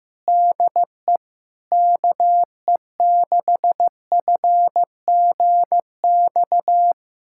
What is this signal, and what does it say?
Morse code, heard as a keyed tone, about 16 wpm, 700 Hz. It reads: DE KE6FGX
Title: USA Callsign DE Callsign - 6 Characters - 15wpm